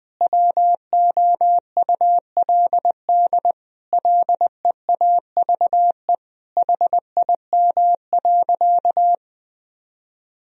Morse code, heard as a keyed tone, about 19 wpm, 700 Hz. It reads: WOULD LEAVE HIM.
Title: Sentences from Top 1000 Words - Review Part 4 - 20wpm